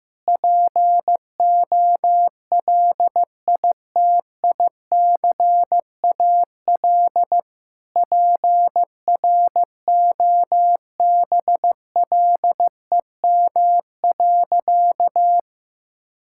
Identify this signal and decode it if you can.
Morse code, heard as a keyed tone, about 15 wpm, 700 Hz. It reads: POLITICAL PROBLEM.
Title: Sentences from Top 1000 Words - Review Part 5 - 15wpm